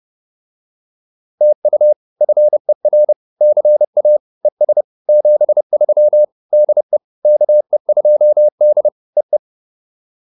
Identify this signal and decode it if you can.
Morse code, heard as a keyed tone, about 30 wpm, 600 Hz. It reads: TU FER CA ES 73 DE KE2D EE